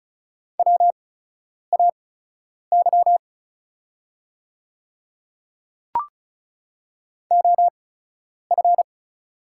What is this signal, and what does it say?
Morse code, heard as a keyed tone, about 36 wpm, spaced as for 10 wpm, 700 Hz. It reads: WAY OF
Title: 8x Character Spacing - Top 100 Words - 35wpm